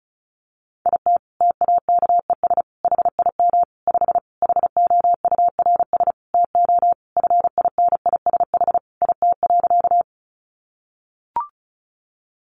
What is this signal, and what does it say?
Morse code, heard as a keyed tone, about 34 wpm, 700 Hz. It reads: IT TAKES HIM 5 HOURS TO FINISH IT.
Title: Additional Easy Phrases 35wpm